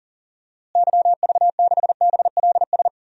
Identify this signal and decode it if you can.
Morse code, heard as a keyed tone, about 37 wpm, 700 Hz. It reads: YV6BLS
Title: DXCC-100 Call Signs - 6 Characters - Repeated 2x - 40wpm